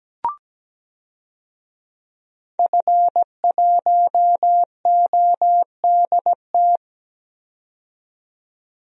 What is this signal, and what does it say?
Morse code, heard as a keyed tone, about 17 wpm, 700 Hz. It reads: F1ODT
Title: DXCC-100 Call Signs - 5 Characters - 17wpm